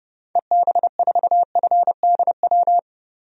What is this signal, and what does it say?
Morse code, heard as a keyed tone, about 29 wpm, 700 Hz. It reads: EB4FDW
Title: DXCC-100 Call Signs - 6 Characters - 30wpm